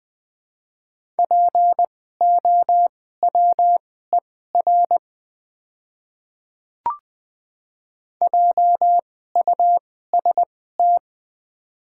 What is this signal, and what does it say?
Morse code, heard as a keyed tone, about 20 wpm, spaced as for 15 wpm, 690 Hz. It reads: POWER JUST
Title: 2x Character Spacing - Top 500 Words - 20wpm